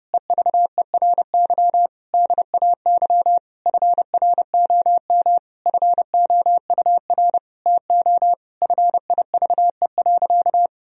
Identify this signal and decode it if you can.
Morse code, heard as a keyed tone, about 30 wpm, 700 Hz. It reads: EVERY DAY FROM FOUR TO FIVE.
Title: Sentences from Top 1000 Words - Review Part 2 - 30wpm